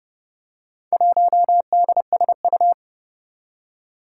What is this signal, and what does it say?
Morse code, heard as a keyed tone, about 30 wpm, 700 Hz. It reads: 1DSU
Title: Four Challenging Letter-Number Combinations 30wpm